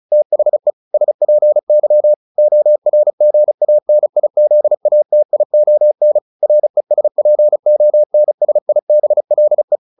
Morse code, heard as a keyed tone, about 34 wpm, 600 Hz. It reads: THE SPY ORGANIZATION RESPONSIBLE